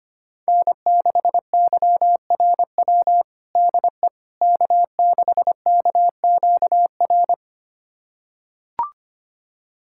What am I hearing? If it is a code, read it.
Morse code, heard as a keyed tone, about 25 wpm, 700 Hz. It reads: N6YRW DE K6KQR
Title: USA Callsign DE Callsign - 5 Characters - 25wpm